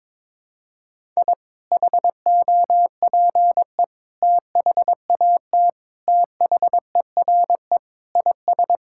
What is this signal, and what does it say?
Morse code, heard as a keyed tone, about 22 wpm, 700 Hz. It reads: I HOPE THAT THERE IS
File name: practice / 22wpm / Sentences with 2-Word Phrases - 22wpm